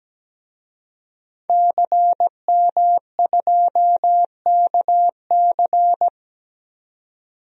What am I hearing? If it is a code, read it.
Morse code, heard as a keyed tone, about 17 wpm, 700 Hz. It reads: CM2KC
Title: DXCC-100 Call Signs - 5 Characters - 17wpm